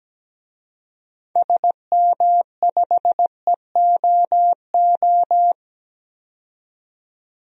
Morse code, heard as a keyed tone, about 17 wpm, 700 Hz. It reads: SM5EOO
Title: DXCC-100 Call Signs - 6 Characters - 17wpm